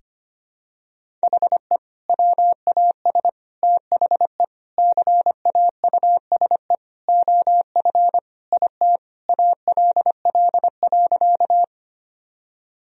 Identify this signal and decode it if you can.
Morse code, heard as a keyed tone, about 25 wpm, 700 Hz. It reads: HE WAS THE CAUSE OF IT ALL.
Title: Sentences from Top 1000 Words - Review Part 1 - 25wpm